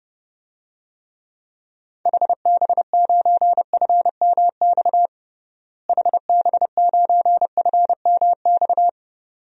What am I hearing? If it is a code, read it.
Morse code, heard as a keyed tone, about 31 wpm, 700 Hz. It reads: HB9FMX HB9FMX
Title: DXCC-100 Call Signs - 6 Characters - Repeated 2x - 30wpm